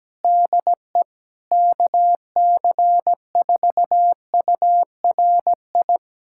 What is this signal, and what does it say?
Morse code, heard as a keyed tone, about 17 wpm, 700 Hz. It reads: DE KC4URI
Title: USA Callsign DE Callsign 17wpm